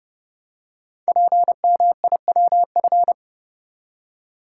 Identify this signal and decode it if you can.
Morse code, heard as a keyed tone, about 30 wpm, 700 Hz. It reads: PMIWF
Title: Code Groups - 5 Letter:Numbers - 30wpm